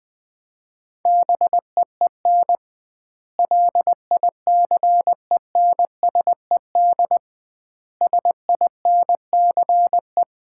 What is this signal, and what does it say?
Morse code, heard as a keyed tone, about 20 wpm, 700 Hz. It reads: BEEN LICENSED SINCE
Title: Common QSO Phrases - 2x Word-Spacing - 20wpm